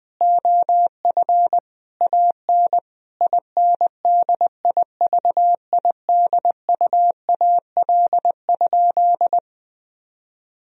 Morse code, heard as a keyed tone, about 20 wpm, 700 Hz. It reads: OF AN INDIVIDUAL?
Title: Sentences from Top 1000 Words - Review Part 4 - 20wpm